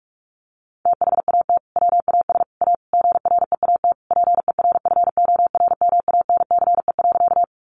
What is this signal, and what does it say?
Morse code, heard as a keyed tone, about 44 wpm, 700 Hz. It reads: THAT WAS A GREAT PERFORMANCE.
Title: Sentences from Top 1000 Words 45wpm